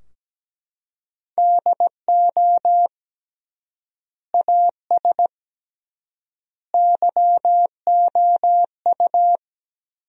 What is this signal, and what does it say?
Morse code, heard as a keyed tone, about 17 wpm, 700 Hz. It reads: DO AS YOU
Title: 3x Word-Spacing - Sentences from Top 100 Words - 17wpm